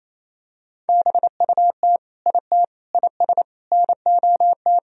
Morse code, heard as a keyed tone, about 28 wpm, 700 Hz. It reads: BUT IT IS NOT
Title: Sentences from Top 1000 Words - Review Part 4 - 28wpm